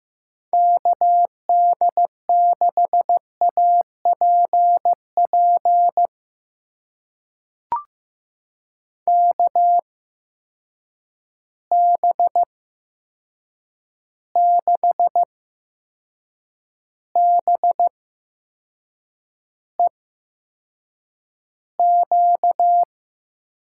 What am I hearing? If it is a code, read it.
Morse code, heard as a keyed tone, about 15 wpm, spaced as for 6 wpm, 700 Hz. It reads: KD6APP KB6BEQ
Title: USA Call Signs - 6 Characters - 8x Character Spacing - 15wpm